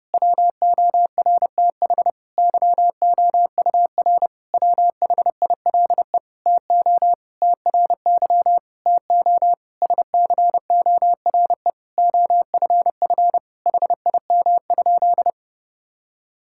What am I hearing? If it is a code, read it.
Morse code, heard as a keyed tone, about 30 wpm, 700 Hz. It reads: WORTH YOUR WHILE TO TRY TO SCORE OFF HIM?
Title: Sentences from Top 1600 Words 30wpm